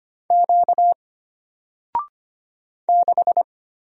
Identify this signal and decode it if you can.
Morse code, heard as a keyed tone, about 25 wpm, 700 Hz. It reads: Q 6
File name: Single-Letter-Number-25wpm